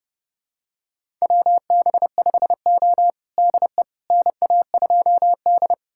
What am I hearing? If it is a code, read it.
Morse code, heard as a keyed tone, about 32 wpm, 700 Hz. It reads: WB5O DE NA2D
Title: USA Callsign DE Callsign - 4 Characters - 30wpm